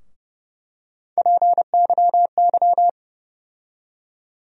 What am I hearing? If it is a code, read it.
Morse code, heard as a keyed tone, about 30 wpm, 700 Hz. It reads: PYY
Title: USA Call Sign Postfixes - 3 Letters 30wpm